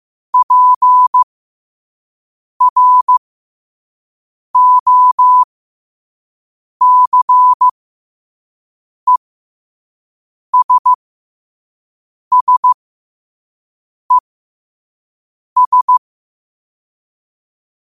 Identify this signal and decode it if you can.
Morse code, heard as a keyed tone, about 15 wpm, spaced as for 5 wpm, 1000 Hz. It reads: PROCESSES
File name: qotdfiles / 5wpm